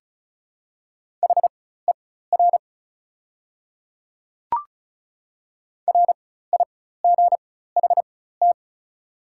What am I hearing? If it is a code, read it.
Morse code, heard as a keyed tone, about 34 wpm, spaced as for 16 wpm, 700 Hz. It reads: HER RIGHT
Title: 4x Character Spacing - Top 500 Words - 35wpm